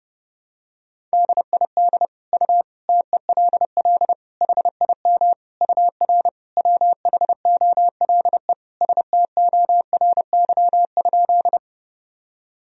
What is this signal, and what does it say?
Morse code, heard as a keyed tone, about 32 wpm, 700 Hz. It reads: DID U TELL HIM UR WHOLE STORY?
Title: Sentences from Top 400 Words and Abbreviations 30wpm